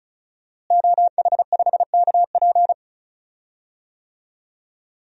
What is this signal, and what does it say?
Morse code, heard as a keyed tone, about 34 wpm, 700 Hz. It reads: OH5KP
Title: DXCC-100 Call Signs - 5 Characters - Repeated 2x - 35wpm